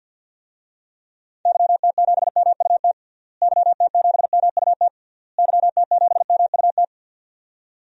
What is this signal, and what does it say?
Morse code, heard as a keyed tone, about 53 wpm, 700 Hz. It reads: YT7MUT YT7MUT YT7MUT
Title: DXCC-100 Call Signs - 6 Characters - Repeated 3x - 50wpm